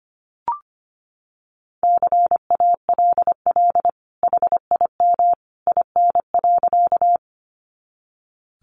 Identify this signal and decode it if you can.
Morse code, heard as a keyed tone, about 25 wpm, 700 Hz. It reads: CALL HIM IN.
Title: Sentences from Top 100 Words 25wpm